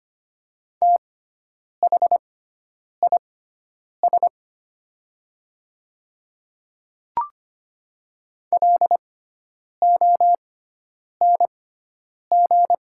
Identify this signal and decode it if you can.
Morse code, heard as a keyed tone, about 25 wpm, spaced as for 9 wpm, 700 Hz. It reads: THIS LONG
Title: 6x Character Spacing - Top 100 Words - 25wpm